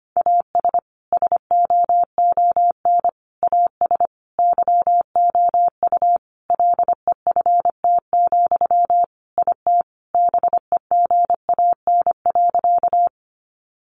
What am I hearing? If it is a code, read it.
Morse code, heard as a keyed tone, about 25 wpm, 700 Hz. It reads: AS SOON AS YOU LEFT, IT BEGAN.
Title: Sentences from Top 400 Words 25wpm